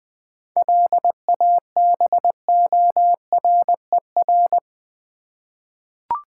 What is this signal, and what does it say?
Morse code, heard as a keyed tone, about 20 wpm, 700 Hz. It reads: LABORER
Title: Practice QSO - Exchange 3 - More Info - 20wpm